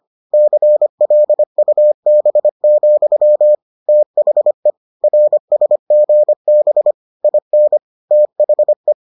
Morse code, heard as a keyed tone, about 25 wpm, 600 Hz. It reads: CLUB, THE RSGB IN THE